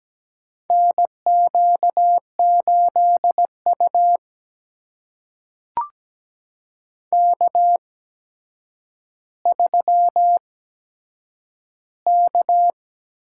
Morse code, heard as a keyed tone, about 17 wpm, spaced as for 8 wpm, 700 Hz. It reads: NQ8U K3K
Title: USA Call Signs - 4 Characters - 8x Character Spacing - 17wpm